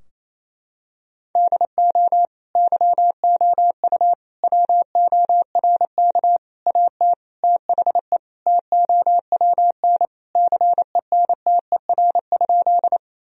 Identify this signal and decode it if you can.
Morse code, heard as a keyed tone, about 28 wpm, 700 Hz. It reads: DO YOU WORK AT THE TOWN CENTER?
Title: Sentences from Top 1000 Words - Review Part 2 - 28wpm